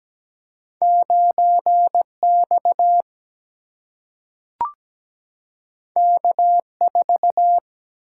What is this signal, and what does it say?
Morse code, heard as a keyed tone, about 17 wpm, 700 Hz. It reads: 9 X K 4
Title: Single-Letter Single-Number 17wpm